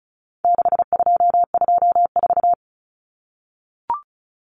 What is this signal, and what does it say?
Morse code, heard as a keyed tone, about 34 wpm, 700 Hz. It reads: 6224
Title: Common Cut Numbers - 4-Digit 35wpm